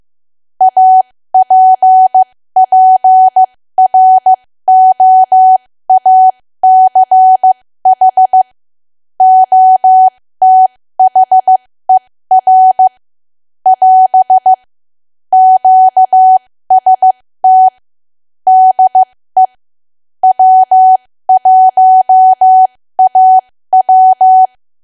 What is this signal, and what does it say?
Morse code, heard as a keyed tone, about 15 wpm, spaced as for 13 wpm, 750 Hz. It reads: APPROACH OTHER <AS> QST DE W1AW
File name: cwa / W1AW-Level2 / 13WPM